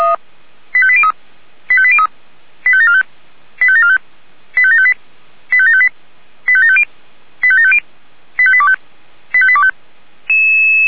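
Five-tone selective call (ZVEI), heard as quick runs of five tones, tones of about 70 ms. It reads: 85803 85803 85745 85745 85758 85758 85750 85750 85825 85825
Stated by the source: ZVEI five-tone selective call received on an SDR.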